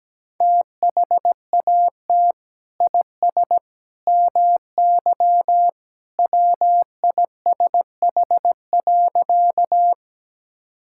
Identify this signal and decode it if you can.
Morse code, heard as a keyed tone, about 17 wpm, 700 Hz. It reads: THAT IS MY WISH.
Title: Sentences from Top 1600 Words 17wpm